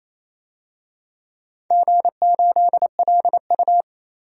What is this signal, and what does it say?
Morse code, heard as a keyed tone, about 28 wpm, 700 Hz. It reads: G8LU
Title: DXCC-100 Call Signs - 4 Characters - 28wpm